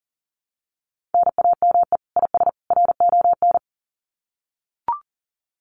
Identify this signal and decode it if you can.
Morse code, heard as a keyed tone, about 40 wpm, 700 Hz. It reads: NAME IS RON
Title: Common QSO Phrases 40wpm